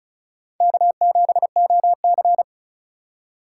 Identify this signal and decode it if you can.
Morse code, heard as a keyed tone, about 34 wpm, 700 Hz. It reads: K7OC
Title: USA Call Signs - 4 Characters - 4x Character Spacing - 35wpm